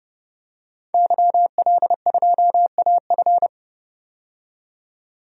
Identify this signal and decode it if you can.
Morse code, heard as a keyed tone, about 30 wpm, 700 Hz. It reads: YL2AF
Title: DXCC-100 Call Signs - 5 Characters - 30wpm